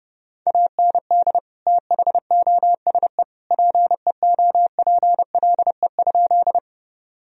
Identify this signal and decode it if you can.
Morse code, heard as a keyed tone, about 30 wpm, 700 Hz. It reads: AND THOSE PEOPLE?
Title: Sentences from Top 1000 Words - Review Part 1 - 30wpm